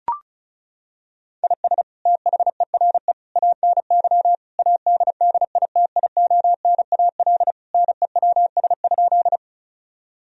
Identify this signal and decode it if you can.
Morse code, heard as a keyed tone, about 34 wpm, 700 Hz. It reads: IS THERE ANY ADDITIONAL NEWS?
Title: Sentences from Top 1000 Words - Review Part 4 - 35wpm